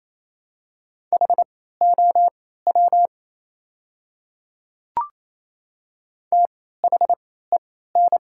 Morse code, heard as a keyed tone, about 27 wpm, spaced as for 16 wpm, 700 Hz. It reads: HOW THEN
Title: 3x Character Spacing - Top 100 Words - 28wpm